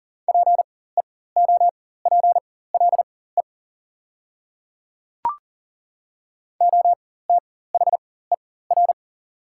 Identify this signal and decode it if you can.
Morse code, heard as a keyed tone, about 40 wpm, spaced as for 19 wpm, 700 Hz. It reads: PEOPLE OTHER
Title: 4x Character Spacing - Top 100 Words - 40wpm